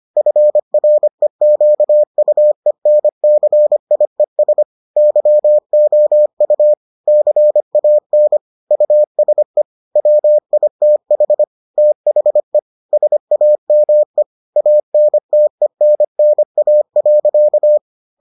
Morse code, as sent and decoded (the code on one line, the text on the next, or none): ..-. .-. --.- ..- . -. -.-. .. ... / -.-- --- ..- / -.-. .- -. / ..- ... / .-- .. - .... / - .... / ... .- -- . / .- -. - . -. -. .- .-.-.-
FREQUENCIES YOU CAN USE WITH THE SAME ANTENNA.